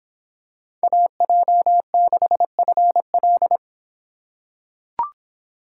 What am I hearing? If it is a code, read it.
Morse code, heard as a keyed tone, about 26 wpm, 700 Hz. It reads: AJ6FL